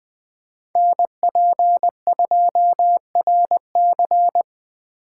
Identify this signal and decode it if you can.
Morse code, heard as a keyed tone, about 20 wpm, 700 Hz. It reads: NP2RC
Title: Speed Racing - USA 2x2 Call Signs 17wpm